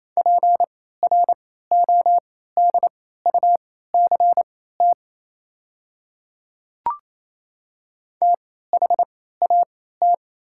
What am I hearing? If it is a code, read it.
Morse code, heard as a keyed tone, about 28 wpm, spaced as for 16 wpm, 700 Hz. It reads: PRODUCT THAT
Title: 3x Character Spacing - Top 500 Words - 28wpm